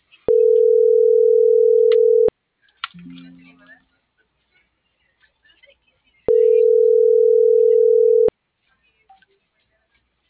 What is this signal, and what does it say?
Ringback tone (North America), 440 + 480 Hz.